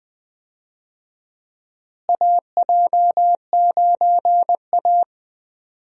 Morse code, heard as a keyed tone, about 20 wpm, 700 Hz. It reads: AJ9A
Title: USA Call Signs - 4 Characters - Repeated 3x - 20wpm